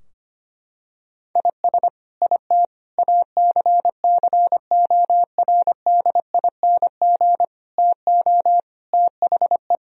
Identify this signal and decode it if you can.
Morse code, heard as a keyed tone, about 25 wpm, 700 Hz. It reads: IS IT ACCORDING TO THE
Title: Sentences from Top 1000 Words - Review Part 4 - 25wpm